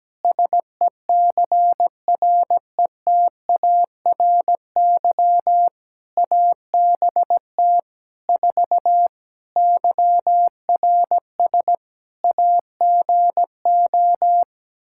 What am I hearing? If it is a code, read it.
Morse code, heard as a keyed tone, about 17 wpm, 700 Hz. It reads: SECRETARY ABT 4 YRS AGO
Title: Practice QSO - Exchange 3 - More Info - 17wpm